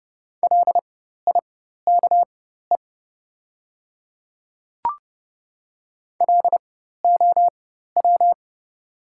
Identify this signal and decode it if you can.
Morse code, heard as a keyed tone, about 30 wpm, spaced as for 14 wpm, 700 Hz. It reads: LIKE LOW
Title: 4x Character Spacing - Top 200 Words - 30wpm